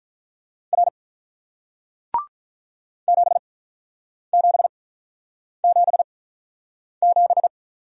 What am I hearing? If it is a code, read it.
Morse code, heard as a keyed tone, about 47 wpm, spaced as for 9 wpm, 700 Hz. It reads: R 7777